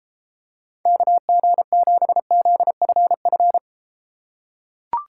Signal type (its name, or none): Morse code